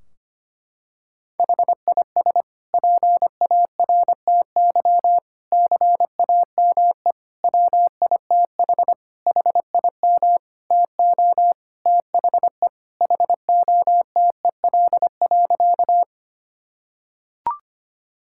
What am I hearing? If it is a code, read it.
Morse code, heard as a keyed tone, about 25 wpm, 700 Hz. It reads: HIS PARTY CAME WITH HIM TO THE HOTEL.
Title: Sentences from Top 1600 Words 25wpm